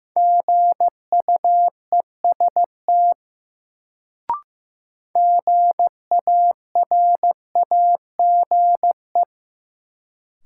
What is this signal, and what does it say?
Morse code, heard as a keyed tone, about 15 wpm, 700 Hz. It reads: GUEST GARAGE
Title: Sentences from Top 1600 Words - New Word Drill Alt - 15wpm